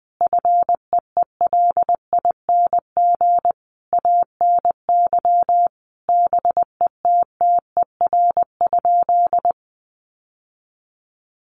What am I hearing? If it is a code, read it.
Morse code, heard as a keyed tone, about 20 wpm, 700 Hz. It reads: FEELING ANY BETTER?
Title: Sentences from Top 1200 Words 20wpm